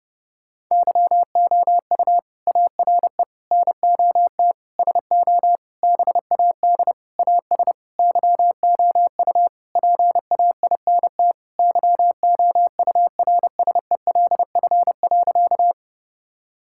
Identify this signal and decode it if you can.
Morse code, heard as a keyed tone, about 30 wpm, 700 Hz. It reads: YOU ARE NOT SO BAD AS YOU PAINT YOURSELF.
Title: Sentences from Top 1000 Words - Review Part 6 - 30wpm